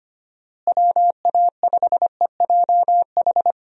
Morse code, heard as a keyed tone, about 25 wpm, 700 Hz. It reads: WA5EJH
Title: USA Call Signs - 6 Characters - Repeated 2x - 25wpm